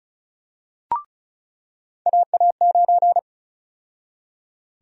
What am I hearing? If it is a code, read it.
Morse code, heard as a keyed tone, about 36 wpm, 700 Hz. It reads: AA9
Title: USA Call Sign Prefixes - 35wpm